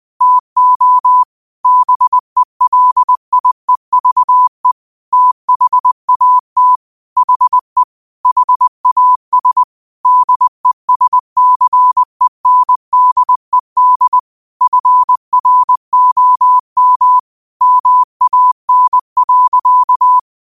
Morse code, heard as a keyed tone, about 20 wpm, 1000 Hz. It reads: TO BELIEVE THAT HE HAS DESCENDED FROM MAN.
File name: qotdfiles / 20wpm